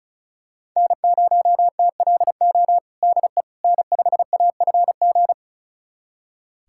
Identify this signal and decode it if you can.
Morse code, heard as a keyed tone, about 34 wpm, 700 Hz. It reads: N0TLO DE N5AFG
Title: USA Callsign DE Callsign - 5 Characters - 35wpm